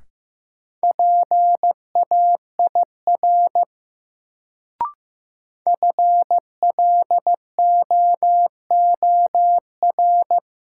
Morse code, heard as a keyed tone, about 15 wpm, 700 Hz. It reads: PAIR FLOOR
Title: Sets of 3 Words - New Word Drill - 15wpm